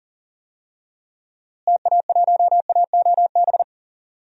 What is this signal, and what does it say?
Morse code, heard as a keyed tone, about 40 wpm, 700 Hz. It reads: TA1AOB
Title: DXCC-100 Call Signs - 6 Characters - Repeated 2x - 40wpm